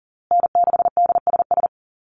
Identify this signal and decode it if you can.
Morse code, heard as a keyed tone, about 40 wpm, 700 Hz. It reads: N6DSS
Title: USA Call Signs Repeated 2x 40wpm